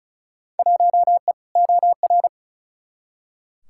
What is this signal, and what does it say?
Morse code, heard as a keyed tone, about 36 wpm, 700 Hz. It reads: 1E OR
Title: Field Day Exchanges - 35wpm